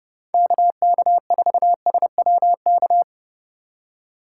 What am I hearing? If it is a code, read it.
Morse code, heard as a keyed tone, about 37 wpm, 700 Hz. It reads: KK4SWK
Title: Speed Racing - USA 2x3 Call Signs 30wpm